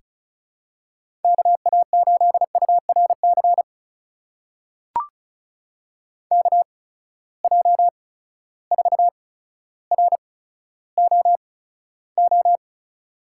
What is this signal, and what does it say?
Morse code, heard as a keyed tone, about 35 wpm, spaced as for 15 wpm, 700 Hz. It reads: KA8URC KJ4ROO
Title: USA Call Signs - 6 Characters - 8x Character Spacing - 35wpm